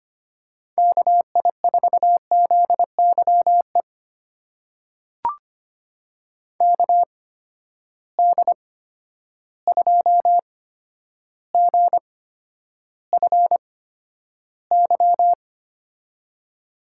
Morse code, heard as a keyed tone, about 25 wpm, spaced as for 11 wpm, 700 Hz. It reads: KI4ZYE KD2GFY
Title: USA Call Signs - 8x Character Spacing - 25wpm